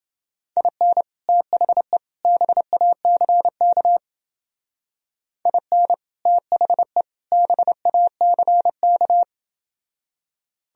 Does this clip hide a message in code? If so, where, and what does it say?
Yes: Morse code from 0.6 to 9.2 s, heard as a keyed tone, about 29 wpm, 700 Hz. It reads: IN THE BACK IN THE BACK